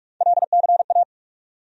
Morse code, heard as a keyed tone, about 46 wpm, 700 Hz. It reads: LCA
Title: USA Call Sign Postfixes - 3 Letters 45wpm